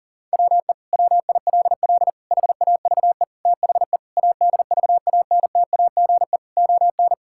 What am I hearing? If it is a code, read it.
Morse code, heard as a keyed tone, about 40 wpm, 700 Hz. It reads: WE WILL HAVE THE ADVANTAGE ON
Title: Sentences from Top 1000 Words - Review Part 6 - 40wpm